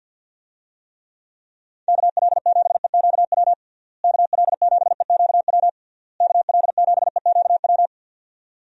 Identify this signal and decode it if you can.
Morse code, heard as a keyed tone, about 52 wpm, 700 Hz. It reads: KL7EQW KL7EQW KL7EQW
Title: DXCC-100 Call Signs - 6 Characters - Repeated 3x - 50wpm